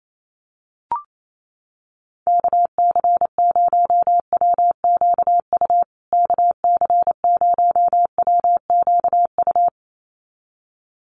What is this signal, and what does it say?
Morse code, heard as a keyed tone, about 28 wpm, 700 Hz. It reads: KC0WQU KC0WQU
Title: USA Call Signs Repeated 2x 28wpm